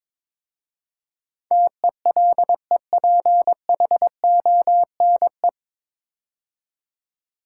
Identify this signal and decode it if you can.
Morse code, heard as a keyed tone, about 22 wpm, 700 Hz. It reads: TELEPHONE
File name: 9 Letter Words - First 3 Letters Given - 22wpm